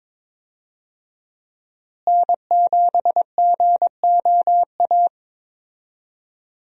Morse code, heard as a keyed tone, about 22 wpm, 700 Hz. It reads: N7GOA
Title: USA Call Signs - 5 Characters - Repeated 2x - 22wpm